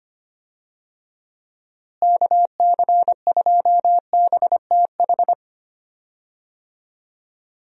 Morse code, heard as a keyed tone, about 25 wpm, 700 Hz. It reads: KC2BTH
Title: USA Call Signs - 6 Characters - Repeated 2x - 25wpm